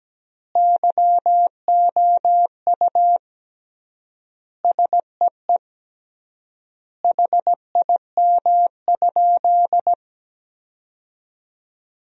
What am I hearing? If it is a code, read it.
Morse code, heard as a keyed tone, about 17 wpm, 700 Hz. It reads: YOU SEE HIM?
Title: 3x Word-Spacing - Sentences from Top 100 Words - 17wpm